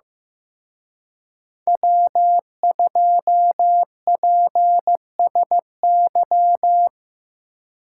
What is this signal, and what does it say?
Morse code, heard as a keyed tone, about 18 wpm, 700 Hz. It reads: W2PSY
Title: Speed Racing - USA 1x3 Call Signs 15wpm